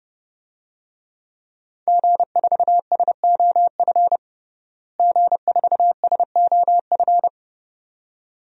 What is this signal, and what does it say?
Morse code, heard as a keyed tone, about 30 wpm, 700 Hz. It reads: G4SOF G4SOF
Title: DXCC-100 Call Signs - 5 Characters - Repeated 2x - 30wpm